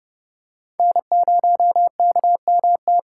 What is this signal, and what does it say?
Morse code, heard as a keyed tone, about 30 wpm, 700 Hz. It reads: N0KMT
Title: USA Call Signs - 5 Characters - 6x Character Spacing - 30wpm